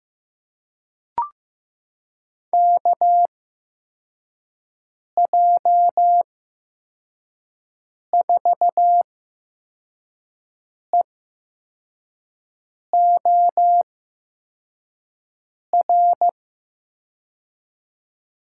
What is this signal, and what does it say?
Morse code, heard as a keyed tone, about 16 wpm, spaced as for 4 wpm, 700 Hz. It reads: KJ4EOR
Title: USA Call Signs - 8x Character Spacing - 15wpm